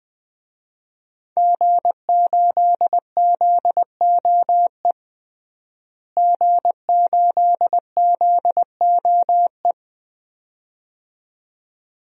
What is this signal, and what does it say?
Morse code, heard as a keyed tone, about 20 wpm, 700 Hz. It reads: G8ZOE G8ZOE
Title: DXCC-100 Call Signs - 5 Characters - Repeated 2x - 20wpm